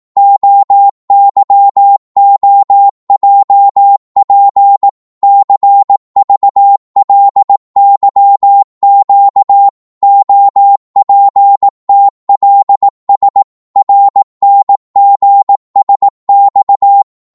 Morse code, heard as a keyed tone, about 18 wpm, 800 Hz. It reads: OYOJP CVLYQ OPTLS RNGSX